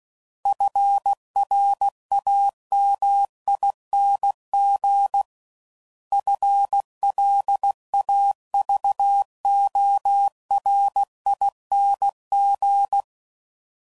Morse code, heard as a keyed tone, about 16 wpm, 800 Hz. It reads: FRAMING FLAVORING
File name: Suffixes / Special INGs / Sngl Ings